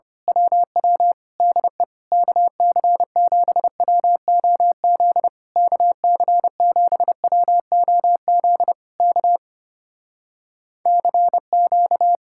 Morse code, heard as a keyed tone, about 24 wpm, 700 Hz. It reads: WW DE KC7WOZ KC7WOZ K CQ
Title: Speed-Racing Common QSO Phrases - 25wpm